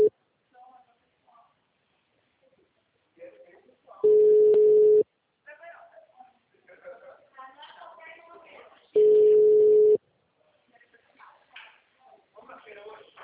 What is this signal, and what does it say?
Ringback tone (Europe, 425 Hz), with speech.